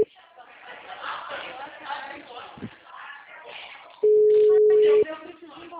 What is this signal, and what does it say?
Ringback tone (Europe, 425 Hz), with speech.
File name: out- -2014- - - .77005